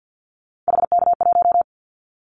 Morse code, heard as a keyed tone, about 50 wpm, 700 Hz. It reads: HK1
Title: Callsign Prefixes for Top 100 DXCC - 50wpm